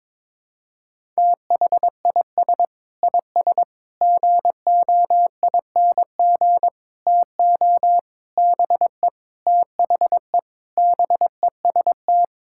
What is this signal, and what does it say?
Morse code, heard as a keyed tone, about 22 wpm, 700 Hz. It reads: THIS IS GOING TO BE THE BEST
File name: practice / 22wpm / Sentences with 4-Word Phrases - 22wpm